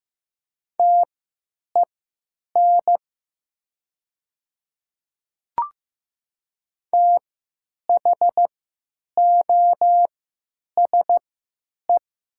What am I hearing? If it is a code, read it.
Morse code, heard as a keyed tone, about 15 wpm, spaced as for 8 wpm, 700 Hz. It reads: TEN THOSE
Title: 3x Character Spacing - Top 500 Words - 15wpm